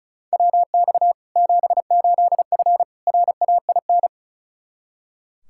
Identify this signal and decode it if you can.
Morse code, heard as a keyed tone, about 36 wpm, 700 Hz. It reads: WX 78F RAIN
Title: K1USN SST QSO Practice 35wpm